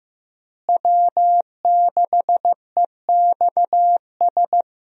Morse code, heard as a keyed tone, about 15 wpm, 700 Hz. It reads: W6EXS
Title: USA Call Signs - 5 Characters - Repeated 3x - 15wpm